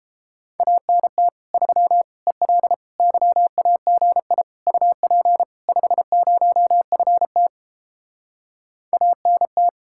Morse code, heard as a keyed tone, about 29 wpm, 700 Hz. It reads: ANT 3 EL YAGI UP 50FT ANT